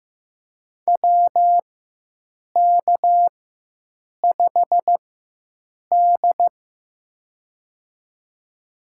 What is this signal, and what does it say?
Morse code, heard as a keyed tone, about 15 wpm, spaced as for 7 wpm, 700 Hz. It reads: WK5D
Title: USA Call Signs - 4x Character Spacing - 15wpm